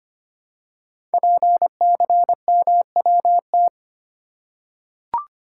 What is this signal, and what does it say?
Morse code, heard as a keyed tone, about 25 wpm, 700 Hz. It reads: PCMWT
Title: Code Groups - 5 Letters - 25wpm